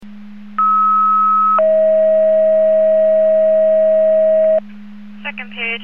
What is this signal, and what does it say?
Two-tone page, heard as a one-second tone, then a three-second tone, about 1290 then 640 Hz, with speech. It is heard from 0.6 to 4.6 s.